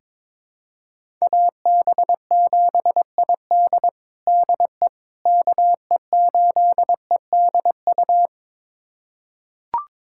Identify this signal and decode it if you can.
Morse code, heard as a keyed tone, about 22 wpm, 700 Hz. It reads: AB7ID DE KE8EDU
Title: USA Callsign DE Callsign 22wpm